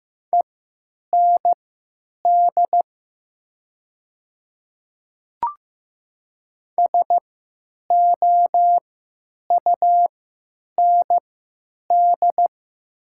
Morse code, heard as a keyed tone, about 15 wpm, spaced as for 9 wpm, 700 Hz. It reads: END SOUND
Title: 3x Character Spacing - Top 300 Words - 15wpm